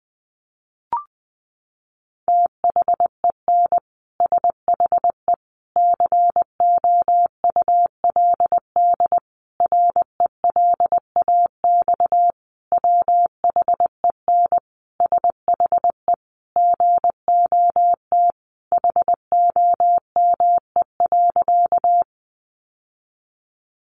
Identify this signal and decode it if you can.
Morse code, heard as a keyed tone, about 20 wpm, 700 Hz. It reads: THEN SHE COULD RELAX WHEN SHE GOT HOME.
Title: Sentences from Top 1400 Words 20wpm